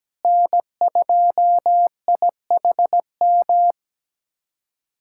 Morse code, heard as a keyed tone, about 17 wpm, 700 Hz. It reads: N2IHM
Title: USA Callsign DE Callsign - 5 Characters - 17wpm